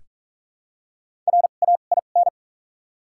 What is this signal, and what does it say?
Morse code, heard as a keyed tone, about 47 wpm, spaced as for 33 wpm, 700 Hz. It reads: RAIN
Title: 2x Character Spacing - Top 300 Words - 45wpm